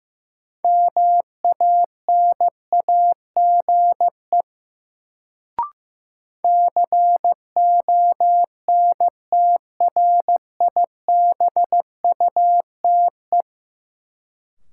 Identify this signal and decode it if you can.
Morse code, heard as a keyed tone, about 15 wpm, 700 Hz. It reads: MANAGE CONTRIBUTE
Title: Sentences from Top 1400 Words - New Word Drill - 15wpm